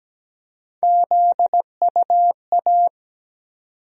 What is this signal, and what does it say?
Morse code, heard as a keyed tone, about 17 wpm, 700 Hz. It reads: ZUA
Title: USA Call Sign Postfixes - 3 Letters 17wpm